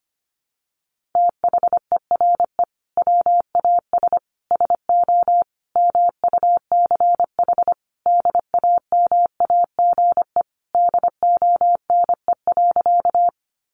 Morse code, heard as a keyed tone, about 25 wpm, 700 Hz. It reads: THERE WAS SO MUCH DAMAGE DONE.
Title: Sentences from Top 1100 Words 25wpm